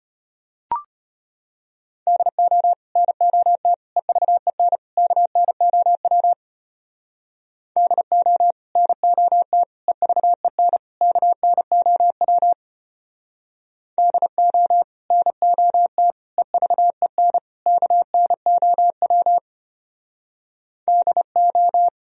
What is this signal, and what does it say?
Morse code, heard as a keyed tone, about 32 wpm, 700 Hz. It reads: DO NOT EVEN KNOW DO NOT EVEN KNOW DO NOT EVEN KNOW DO